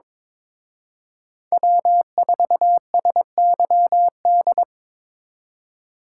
Morse code, heard as a keyed tone, about 26 wpm, 700 Hz. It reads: W4SYD
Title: Speed Racing - USA 1x3 Call Signs 22wpm